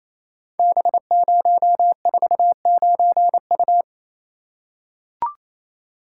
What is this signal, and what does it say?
Morse code, heard as a keyed tone, about 27 wpm, 700 Hz. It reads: B049U
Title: Code Groups - 5 Letter:Numbers - 28wpm